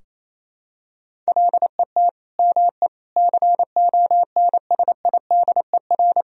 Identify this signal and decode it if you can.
Morse code, heard as a keyed tone, about 28 wpm, 700 Hz. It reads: LET ME CONSIDER
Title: Sentences from Top 1000 Words - Review Part 4 - 28wpm